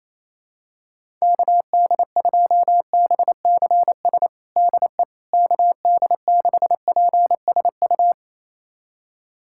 Morse code, heard as a keyed tone, about 28 wpm, 700 Hz. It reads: KD2BCS DE KD6PSU
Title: USA Callsign DE Callsign 28wpm